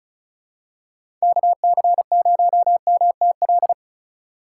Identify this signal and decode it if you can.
Morse code, heard as a keyed tone, about 34 wpm, 700 Hz. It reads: KC0MTL
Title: USA Call Signs - 6 Characters - 2x Character Spacing - 35wpm